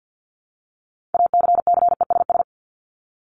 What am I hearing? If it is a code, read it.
Morse code, heard as a keyed tone, about 50 wpm, 700 Hz. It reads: ACCESS
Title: Top 501-750 Words 50wpm